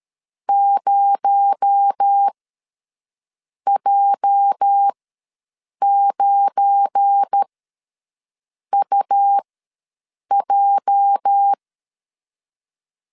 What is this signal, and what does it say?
Morse code, heard as a keyed tone, about 13 wpm, spaced as for 6 wpm, 800 Hz. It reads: 0J9UJ
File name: LEZIONI / 8 YJ45 / 8 YJ ws 12wpm 3342